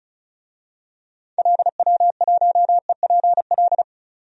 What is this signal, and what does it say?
Morse code, heard as a keyed tone, about 34 wpm, 700 Hz. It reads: LW1EPL
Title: DXCC-100 Call Signs - 6 Characters - 35wpm